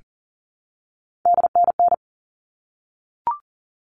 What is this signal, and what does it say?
Morse code, heard as a keyed tone, about 40 wpm, 700 Hz. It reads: DNN
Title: All Cut Numbers - 3-Digit 40wpm